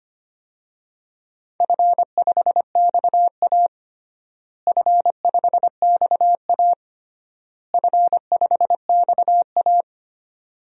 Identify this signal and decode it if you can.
Morse code, heard as a keyed tone, about 25 wpm, 700 Hz. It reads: F5XA F5XA F5XA
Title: DXCC-100 Call Signs - 4 Characters - Repeated 3x - 25wpm